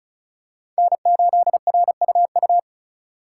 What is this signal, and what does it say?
Morse code, heard as a keyed tone, about 36 wpm, 700 Hz. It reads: N8RUU
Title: USA Call Signs - 5 Characters - 4x Character Spacing - 35wpm